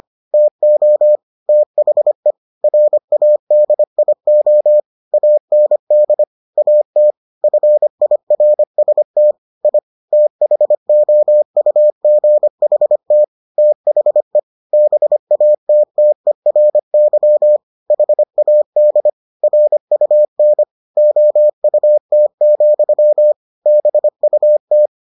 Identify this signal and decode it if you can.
Morse code, heard as a keyed tone, about 25 wpm, 600 Hz. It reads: TO THE RADIO AND AT FIRST I THOUGHT THE BATTERY HAD RUN OUT, BUT